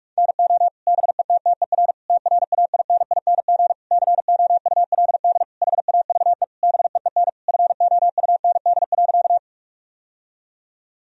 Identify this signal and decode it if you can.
Morse code, heard as a keyed tone, about 43 wpm, 700 Hz. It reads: NO BETTER TRAINING COULD HAVE BEEN FOUND.
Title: Sentences from Top 1000 Words - Review Part 4 - 45wpm